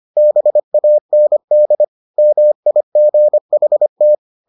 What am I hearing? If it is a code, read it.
Morse code, heard as a keyed tone, about 25 wpm, 600 Hz. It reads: BAND MIGHT